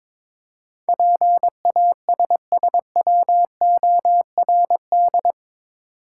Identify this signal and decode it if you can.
Morse code, heard as a keyed tone, about 22 wpm, 700 Hz. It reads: PASSWORD
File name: practice / 22wpm / Eight Letter Words - 22wpm